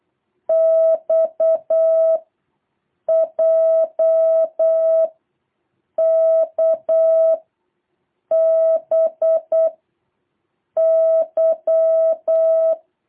Morse code, heard as a keyed tone, about 8 wpm, 650 Hz. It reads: X J K B Y
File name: jh8cbh / 2023cw